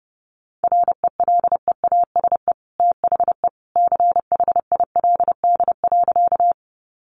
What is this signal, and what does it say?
Morse code, heard as a keyed tone, about 28 wpm, 700 Hz. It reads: RELEASE THE CHILD.
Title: Sentences from Top 1100 Words 30wpm